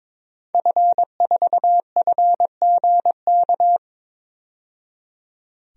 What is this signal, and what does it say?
Morse code, heard as a keyed tone, about 22 wpm, 700 Hz. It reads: F4FGK
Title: DXCC-100 Call Signs - 5 Characters - 22wpm